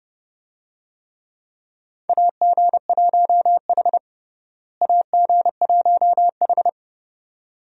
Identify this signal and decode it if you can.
Morse code, heard as a keyed tone, about 30 wpm, 700 Hz. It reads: AG1H AG1H
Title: USA Call Signs - 4 Characters - Repeated 2x - 30wpm